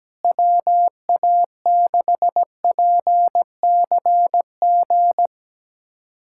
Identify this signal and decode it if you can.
Morse code, heard as a keyed tone, about 17 wpm, 700 Hz. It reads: WA6PCG
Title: USA Callsign DE Callsign 17wpm